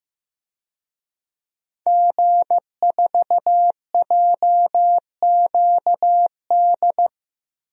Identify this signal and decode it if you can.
Morse code, heard as a keyed tone, about 16 wpm, 700 Hz. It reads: G4JQD
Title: DXCC-100 Call Signs - 5 Characters - 15wpm